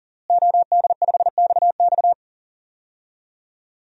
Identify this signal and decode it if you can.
Morse code, heard as a keyed tone, about 40 wpm, 700 Hz. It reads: OD5XX
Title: DXCC-100 Call Signs - 5 Characters - Repeated 2x - 40wpm